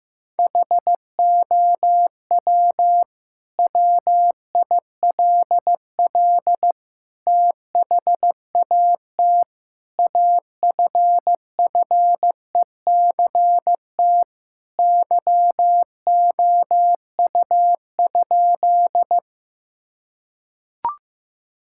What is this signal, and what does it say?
Morse code, heard as a keyed tone, about 15 wpm, 700 Hz. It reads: HOW WILL THAT AFFECT YOU?
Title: Sentences from Top 1000 Words - Review Part 6 - 15wpm